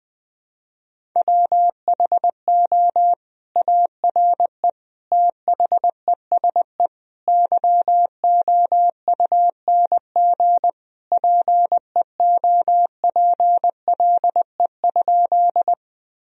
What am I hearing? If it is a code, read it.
Morse code, heard as a keyed tone, about 21 wpm, 700 Hz. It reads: WHO ARE THESE YOUNG PEOPLE?
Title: Sentences from Top 1000 Words - Review Part 1 - 20wpm